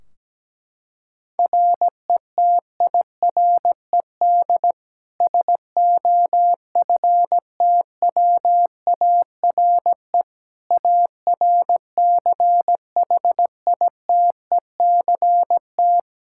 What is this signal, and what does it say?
Morse code, heard as a keyed tone, about 17 wpm, 700 Hz. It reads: RETIRED SOFTWARE ARCHITECT
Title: Practice QSO - Exchange 3 - More Info - 17wpm